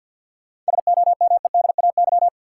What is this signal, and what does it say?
Morse code, heard as a keyed tone, about 51 wpm, 700 Hz. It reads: SOMEDAY
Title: Sentences from Top 1600 Words - New Word Drill - 50wpm